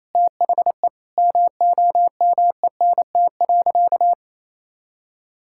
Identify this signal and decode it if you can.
Morse code, heard as a keyed tone, about 28 wpm, 700 Hz. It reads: THE MOMENT.
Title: Sentences from Top 1000 Words - Review Part 6 - 28wpm